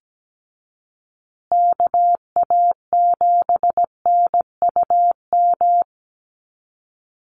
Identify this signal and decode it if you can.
Morse code, heard as a keyed tone, about 17 wpm, 700 Hz. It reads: KA7NUM
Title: USA Call Signs Repeated 2x 17wpm